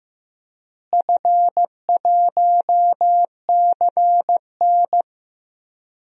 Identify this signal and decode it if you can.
Morse code, heard as a keyed tone, about 15 wpm, 700 Hz. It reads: F1CN
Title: DXCC-100 Call Signs - 4 Characters - Repeated 3x - 15wpm